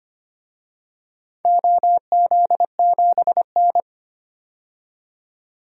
Morse code, heard as a keyed tone, about 25 wpm, 700 Hz. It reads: OZ7N
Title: DXCC-100 Call Signs - 4 Characters - 25wpm